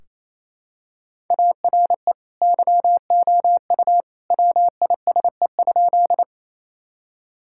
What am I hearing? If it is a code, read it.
Morse code, heard as a keyed tone, about 28 wpm, 700 Hz. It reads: ARE YOU WISE?
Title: Sentences from Top 1600 Words 28wpm